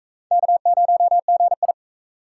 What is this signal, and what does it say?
Morse code, heard as a keyed tone, about 37 wpm, 700 Hz. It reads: K0GI